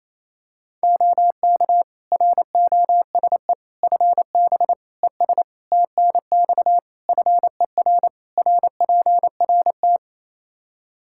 Morse code, heard as a keyed tone, about 28 wpm, 700 Hz. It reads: OK ROSE FB ES TNX FER RPRT
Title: Practice QSO - Exchange 2 - Info - 28wpm